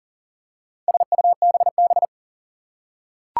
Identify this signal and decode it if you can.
Morse code, heard as a keyed tone, about 40 wpm, 690 Hz. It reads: SUBB
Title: Four Challenging Letter-Number Combinations 40wpm